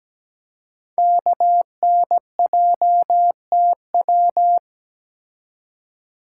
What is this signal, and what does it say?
Morse code, heard as a keyed tone, about 17 wpm, 700 Hz. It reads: KNJTW
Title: Code Groups - 5 Letter:Numbers - 17wpm